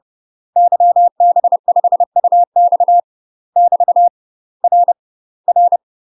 Morse code, heard as a keyed tone, about 30 wpm, 700 Hz. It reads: YB5UX <BT> R R
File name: UK MASTER QSOs TRAINING / 30WPM / UK MASTER QSOs 0023